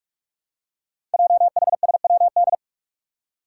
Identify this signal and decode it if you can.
Morse code, heard as a keyed tone, about 44 wpm, 700 Hz. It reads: JHSWD
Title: Code Groups - 5 Letters - 45wpm